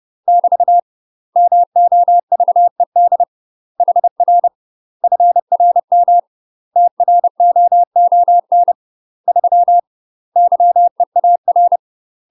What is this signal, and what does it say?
Morse code, heard as a keyed tone, about 30 wpm, 700 Hz. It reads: <BT> MOVED HR FRM TROON 3 YEAR
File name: UK MASTER QSOs TRAINING / 30WPM / UK MASTER QSOs 0017